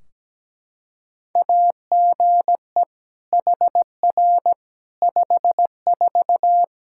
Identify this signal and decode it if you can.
Morse code, heard as a keyed tone, about 17 wpm, 700 Hz. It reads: AGE HR 54
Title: Practice QSO - Exchange 3 - More Info - 17wpm